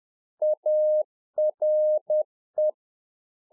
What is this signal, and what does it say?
Morse code, heard as a keyed tone, about 10 wpm, 610 Hz. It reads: ARE
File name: 2011-04-mp3 / RICK JJ0MPI 10WPM